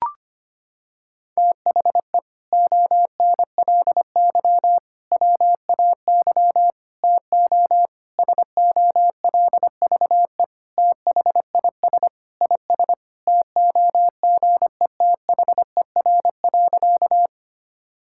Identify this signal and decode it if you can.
Morse code, heard as a keyed tone, about 25 wpm, 700 Hz. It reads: THE ONLY WAY TO SOLVE THIS IS TOGETHER.
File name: practice / 25wpm / Sentences with 4-Word Phrases - 25wpm